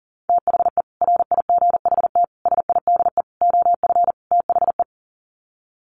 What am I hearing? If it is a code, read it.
Morse code, heard as a keyed tone, about 40 wpm, 700 Hz. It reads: THE RIGHT SIDE OF THE
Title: Sets of 5 Words Encore 40wpm